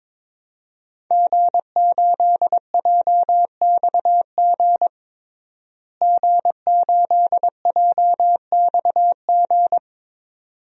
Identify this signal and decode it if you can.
Morse code, heard as a keyed tone, about 21 wpm, 700 Hz. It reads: G8JXG G8JXG
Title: DXCC-100 Call Signs - 5 Characters - Repeated 2x - 22wpm